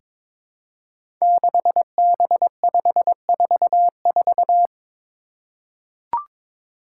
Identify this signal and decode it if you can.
Morse code, heard as a keyed tone, about 22 wpm, 700 Hz. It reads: 6B544
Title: Five Challenging Letter-Number Combinations 22wpm